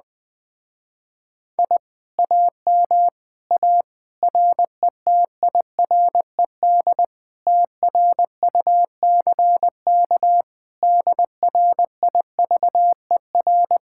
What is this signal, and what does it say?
Morse code, heard as a keyed tone, about 20 wpm, 700 Hz. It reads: I AM A RETIRED TRUCK DRIVER
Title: Speed-Racing Common QSO Phrases - 17wpm